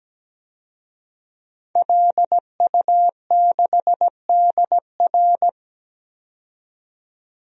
Morse code, heard as a keyed tone, about 17 wpm, 700 Hz. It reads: LU6DR
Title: DXCC-100 Call Signs - 5 Characters - 17wpm